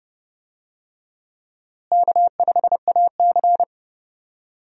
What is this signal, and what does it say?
Morse code, heard as a keyed tone, about 30 wpm, 700 Hz. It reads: K5AC
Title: USA Call Signs - 4 Characters - Repeated 2x - 30wpm